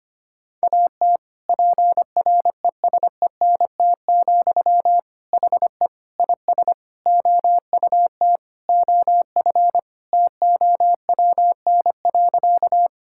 Morse code, heard as a keyed tone, about 25 wpm, 700 Hz. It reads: AT PRESENT, HE IS OUT OF TOWN.
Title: Sentences from Top 1000 Words - Review Part 5 - 25wpm